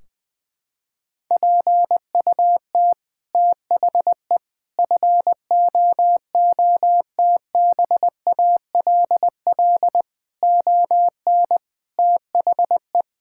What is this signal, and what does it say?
Morse code, heard as a keyed tone, about 21 wpm, 700 Hz. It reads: PUT THE FOOTBALL ON THE
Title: Sentences from Top 1600 Words 20wpm